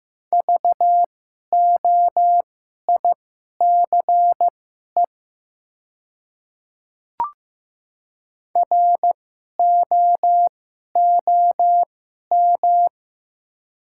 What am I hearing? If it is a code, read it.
Morse code, heard as a keyed tone, about 15 wpm, spaced as for 11 wpm, 700 Hz. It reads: VOICE ROOM
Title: 2x Character Spacing - Top 500 Words - 15wpm